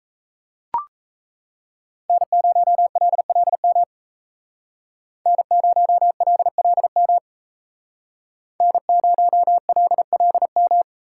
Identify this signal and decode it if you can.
Morse code, heard as a keyed tone, about 35 wpm, 700 Hz. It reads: N0LLM N0LLM N0LLM
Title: Speed Racing - USA 1x3 Call Signs 28wpm